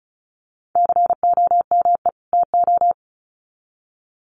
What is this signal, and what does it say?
Morse code, heard as a keyed tone, about 34 wpm, 700 Hz. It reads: COME TO
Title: Sets of 2 Words 35wpm